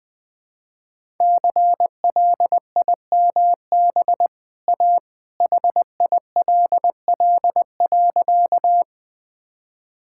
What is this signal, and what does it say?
Morse code, heard as a keyed tone, about 20 wpm, 700 Hz. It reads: CLIMB A HILL.
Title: Easy Sentences with Hints - 20wpm